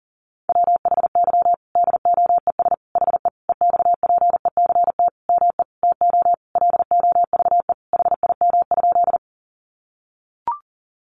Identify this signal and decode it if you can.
Morse code, heard as a keyed tone, about 40 wpm, 700 Hz. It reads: WHY DOES HE EXPECT ME TO LOVE HIM?
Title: Sentences from Top 1100 Words 40wpm